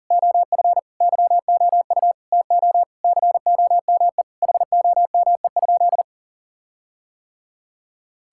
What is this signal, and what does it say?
Morse code, heard as a keyed tone, about 37 wpm, 700 Hz. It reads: OF YOU TO COME HOME?
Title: Sentences from Top 1600 Words 40wpm